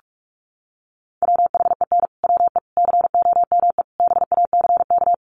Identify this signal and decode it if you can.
Morse code, heard as a keyed tone, about 44 wpm, 700 Hz. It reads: WHEN WE COME BACK
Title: Sets of 4 Words 45wpm